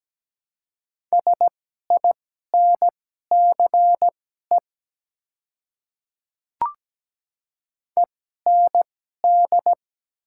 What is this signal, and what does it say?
Morse code, heard as a keyed tone, about 17 wpm, spaced as for 12 wpm, 700 Hz. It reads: SINCE END
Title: 2x Character Spacing - Top 300 Words - 17wpm